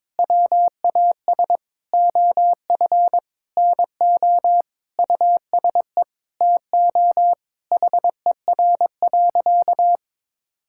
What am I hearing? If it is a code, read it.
Morse code, heard as a keyed tone, about 22 wpm, 700 Hz. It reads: WAS OF NO USE TO HER.
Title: Sentences from Top 1000 Words - Review Part 6 - 22wpm